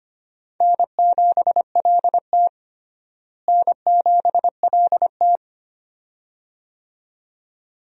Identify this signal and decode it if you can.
Morse code, heard as a keyed tone, about 25 wpm, 700 Hz. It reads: N7LT N7LT
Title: USA Call Signs - 4 Characters - Repeated 2x - 25wpm